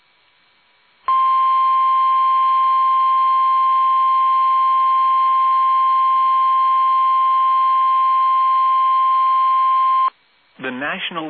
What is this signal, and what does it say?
Emergency Alert System (SAME) tones (the 1050 Hz attention tone), with speech. It is heard from 1.0 to 10.1 s.